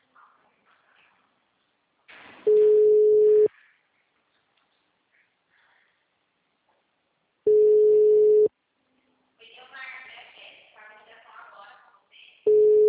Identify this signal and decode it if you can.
Ringback tone (Europe, 425 Hz), with speech.